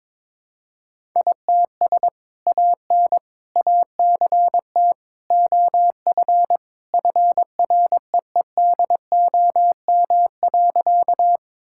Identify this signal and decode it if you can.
Morse code, heard as a keyed tone, about 22 wpm, 700 Hz. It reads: ITS AN ACT OF FREEDOM.
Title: Sentences from Top 1000 Words - Review Part 5 - 22wpm